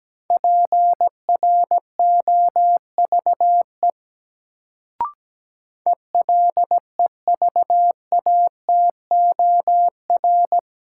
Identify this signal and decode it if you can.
Morse code, heard as a keyed tone, about 17 wpm, 700 Hz. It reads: PROVE ELEVATOR
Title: Sentences from Top 1600 Words - New Word Drill - 17wpm